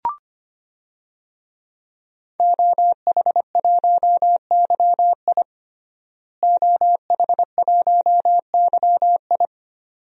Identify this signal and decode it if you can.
Morse code, heard as a keyed tone, about 25 wpm, 700 Hz. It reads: OH1YI OH1YI
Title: DXCC-100 Call Signs - 5 Characters - Repeated 2x - 25wpm